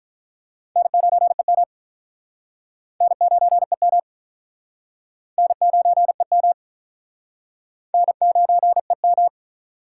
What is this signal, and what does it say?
Morse code, heard as a keyed tone, about 44 wpm, 700 Hz. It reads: N9EM N9EM N9EM N9EM
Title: Speed Racing - USA 1x2 Call Signs 35wpm